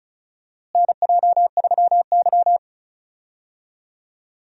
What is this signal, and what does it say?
Morse code, heard as a keyed tone, about 36 wpm, 700 Hz. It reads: NJ3Y
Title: USA Call Signs - 4 Characters - Repeated 3x - 35wpm